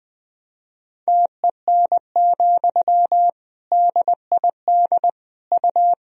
Morse code, heard as a keyed tone, about 20 wpm, 700 Hz. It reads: TEN, DID U
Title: Sentences from Top 400 Words and Abbreviations 20wpm